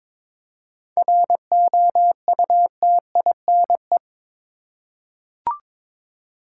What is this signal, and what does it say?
Morse code, heard as a keyed tone, about 23 wpm, 700 Hz. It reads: ROUTINE
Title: Seven Letter Words - 22wpm